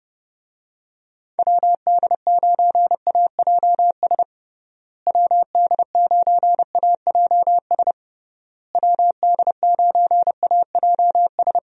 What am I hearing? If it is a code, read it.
Morse code, heard as a keyed tone, about 30 wpm, 700 Hz. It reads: WD9AJS WD9AJS WD9AJS
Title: USA Call Signs - 6 Characters - Repeated 3x - 30wpm